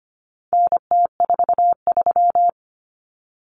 Morse code, heard as a keyed tone, about 25 wpm, 700 Hz. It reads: NT43
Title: Common Cut Numbers - 4-Digit 25wpm